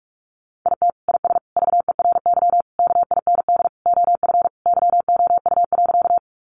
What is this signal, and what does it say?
Morse code, heard as a keyed tone, about 44 wpm, 700 Hz. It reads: IT IS VERY KIND OF YOU.
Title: Sentences from Top 200 Words 45wpm